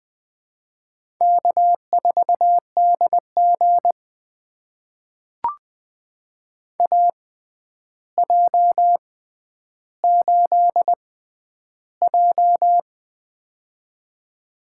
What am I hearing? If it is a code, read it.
Morse code, heard as a keyed tone, about 20 wpm, spaced as for 10 wpm, 700 Hz. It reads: K4DG AJ8J
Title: USA Call Signs - 4 Characters - 6x Character Spacing - 20wpm